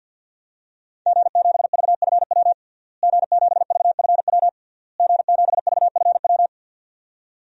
Morse code, heard as a keyed tone, about 50 wpm, 700 Hz. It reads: G7VFW G7VFW G7VFW
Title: DXCC-100 Call Signs - 5 Characters - Repeated 3x - 50wpm